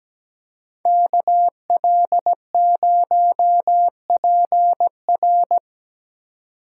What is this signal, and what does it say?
Morse code, heard as a keyed tone, about 20 wpm, 700 Hz. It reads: KL0PR
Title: Speed Racing - USA 2x2 Call Signs 17wpm